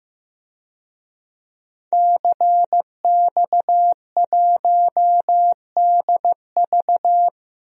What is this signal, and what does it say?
Morse code, heard as a keyed tone, about 15 wpm, 700 Hz. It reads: CX1DV
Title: DXCC-100 Call Signs - 5 Characters - 15wpm